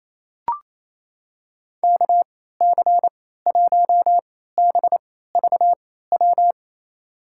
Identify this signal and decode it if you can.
Morse code, heard as a keyed tone, about 27 wpm, spaced as for 16 wpm, 700 Hz. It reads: KC1BVW
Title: USA Call Signs - 6 Characters - 3x Character Spacing - 28wpm